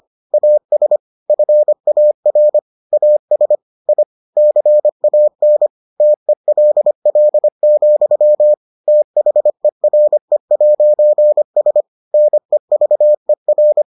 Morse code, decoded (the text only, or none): AS FAR AS I CAN TELL, THERE'S NEVER